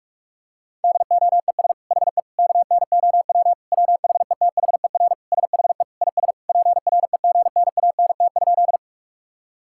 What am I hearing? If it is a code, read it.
Morse code, heard as a keyed tone, about 45 wpm, 700 Hz. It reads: DOES HE KNOW WHETHER SHE IS PREGNANT?
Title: Sentences from Top 1000 Words - Review Part 6 - 45wpm